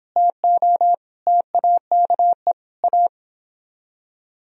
Morse code, heard as a keyed tone, about 29 wpm, 700 Hz. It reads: TO TAKE A